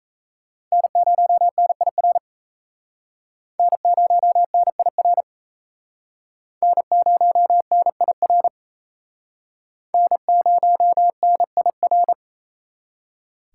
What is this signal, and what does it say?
Morse code, heard as a keyed tone, about 35 wpm, 700 Hz. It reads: N0NIR N0NIR N0NIR N0NIR
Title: Speed Racing - USA Call Signs 28wpm